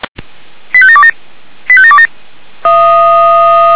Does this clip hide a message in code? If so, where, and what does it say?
Yes: five-tone selective call (ZVEI) from 0.7 to 2.1 s, heard as quick runs of five tones, tones of about 70 ms. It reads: 85727 85727